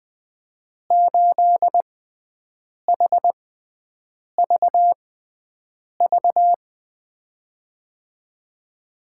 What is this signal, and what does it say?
Morse code, heard as a keyed tone, about 21 wpm, spaced as for 7 wpm, 700 Hz. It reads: 8HVV
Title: USA Call Signs - 6 Characters - 6x Character Spacing - 20wpm